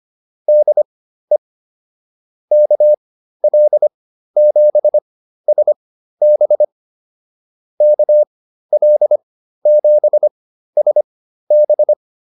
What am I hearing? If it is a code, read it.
Morse code, heard as a keyed tone, about 25 wpm, spaced as for 13 wpm, 600 Hz. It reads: DE KL7SB KL7SB